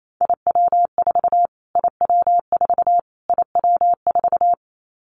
Morse code, heard as a keyed tone, about 28 wpm, 700 Hz. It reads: IW4 IW4 IW4
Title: Callsign Prefixes for Top 25 DXCC - Repeated 3x - 28wpm